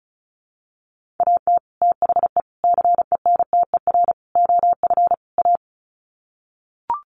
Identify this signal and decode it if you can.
Morse code, heard as a keyed tone, about 34 wpm, 700 Hz. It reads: AT THE CENTER OF A
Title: Sets of 5 Words Encore2 35wpm